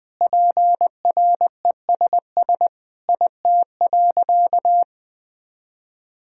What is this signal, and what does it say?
Morse code, heard as a keyed tone, about 20 wpm, 700 Hz. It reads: PRESS IT.
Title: Sentences from Top 1000 Words - Review Part 1 - 20wpm